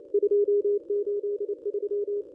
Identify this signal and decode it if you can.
Morse code, heard as a keyed tone, about 28 wpm, 410 Hz. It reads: 283
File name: contests / ss-cw-k5tr-op-n5rz -1200